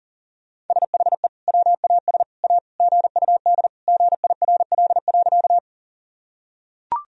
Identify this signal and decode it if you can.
Morse code, heard as a keyed tone, about 40 wpm, 700 Hz. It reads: SHE WAS A GUD GIRL.
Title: Sentences from Top 400 Words and Abbreviations 40wpm